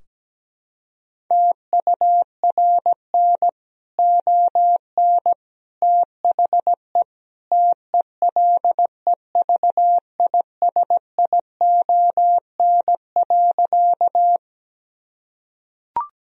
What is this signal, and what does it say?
Morse code, heard as a keyed tone, about 17 wpm, 700 Hz. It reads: TURN ON THE TELEVISION.
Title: Sentences from Top 1000 Words - Review Part 4 - 17wpm